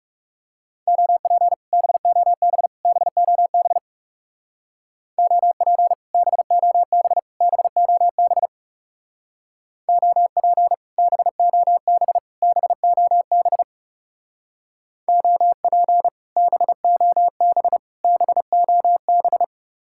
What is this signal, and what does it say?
Morse code, heard as a keyed tone, about 37 wpm, 700 Hz. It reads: OP BOB BOB OP BOB BOB OP BOB BOB OP BOB BOB
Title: Speed-Racing Common QSO Phrases - 30wpm